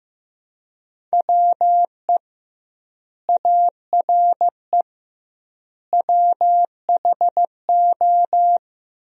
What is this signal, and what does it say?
Morse code, heard as a keyed tone, about 15 wpm, 700 Hz. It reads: WE ARE WHO
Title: 2x Word-Spacing - Sentences from Top 200 Words - 15wpm